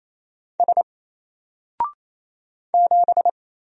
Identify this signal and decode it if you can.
Morse code, heard as a keyed tone, about 27 wpm, 700 Hz. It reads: S 7
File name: Single-Letter-Number-28wpm